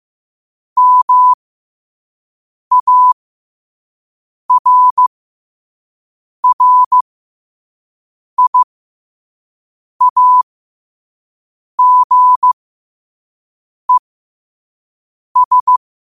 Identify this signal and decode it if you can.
Morse code, heard as a keyed tone, about 15 wpm, spaced as for 5 wpm, 1000 Hz. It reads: MARRIAGES
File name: qotdfiles / 5wpm